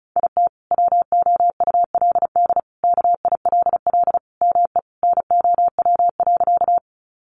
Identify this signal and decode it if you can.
Morse code, heard as a keyed tone, about 34 wpm, 700 Hz. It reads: IT WOULD KILL ME NOW.
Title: Sentences from Top 1400 Words 35wpm